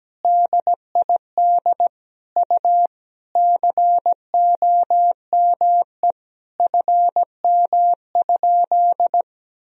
Morse code, heard as a keyed tone, about 17 wpm, 700 Hz. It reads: DID U COME FM?
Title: Sentences from Top 200 Words and Abbreviations 17wpm